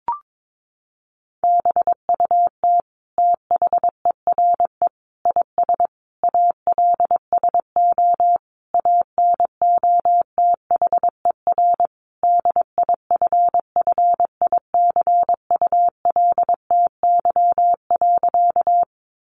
Morse code, heard as a keyed tone, about 22 wpm, 700 Hz. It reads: BUT THERE IS ALSO ANOTHER DIFFICULTY.
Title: Sentences from Top 1500 Words 22wpm